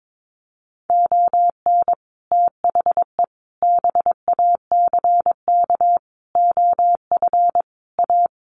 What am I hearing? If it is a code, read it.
Morse code, heard as a keyed tone, about 22 wpm, 700 Hz. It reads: ON THE BACK OF A
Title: Sets of 5 Words Encore2 22wpm